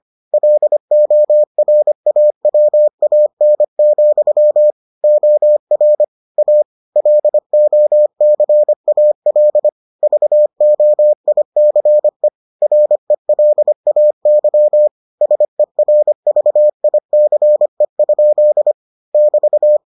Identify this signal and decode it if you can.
Morse code, heard as a keyed tone, about 25 wpm, 600 Hz. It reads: LORAWAN, OR A LOCAL VOICE RELAY SERVICE? <BT>